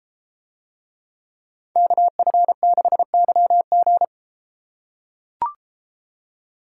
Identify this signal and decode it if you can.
Morse code, heard as a keyed tone, about 33 wpm, 700 Hz. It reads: KF6YG